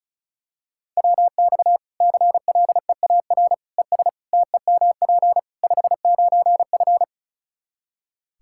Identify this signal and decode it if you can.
Morse code, heard as a keyed tone, about 34 wpm, 700 Hz. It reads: WX CLEAR ES TEMP 59F
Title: Practice QSO - Exchange 2 - Info - 35wpm